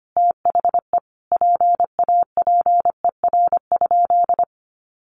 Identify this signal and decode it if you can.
Morse code, heard as a keyed tone, about 25 wpm, 700 Hz. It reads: THE PAPER?
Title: Sentences from Top 1300 Words 25wpm